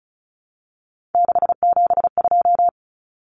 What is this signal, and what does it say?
Morse code, heard as a keyed tone, about 34 wpm, 700 Hz. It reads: 672
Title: Numbers - 3-Digit 35wpm